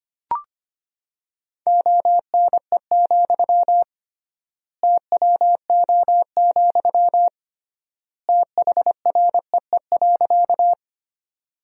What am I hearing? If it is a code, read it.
Morse code, heard as a keyed tone, about 25 wpm, 700 Hz. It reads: ONE, TWO, THREE.
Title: 3x Word-Spacing - Sentences from Top 200 Words - 25wpm